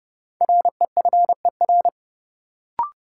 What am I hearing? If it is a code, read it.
Morse code, heard as a keyed tone, about 30 wpm, 700 Hz. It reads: REFER
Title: Sentences from Top 1000 Words - New Word Drill - 30wpm